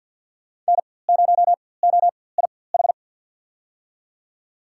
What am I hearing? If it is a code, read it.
Morse code, heard as a keyed tone, about 53 wpm, spaced as for 24 wpm, 700 Hz. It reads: N0OIH
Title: USA Call Signs - 5 Characters - 4x Character Spacing - 50wpm